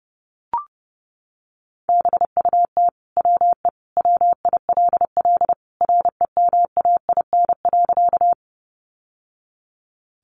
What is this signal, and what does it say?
Morse code, heard as a keyed tone, about 30 wpm, 700 Hz. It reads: BUT WE WILL REMAIN.
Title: Sentences from Top 1200 Words 30wpm